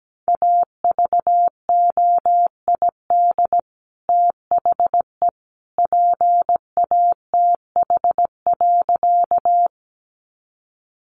Morse code, heard as a keyed tone, about 17 wpm, 700 Hz. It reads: AVOID THE PATH.
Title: Sentences from Top 1400 Words 17wpm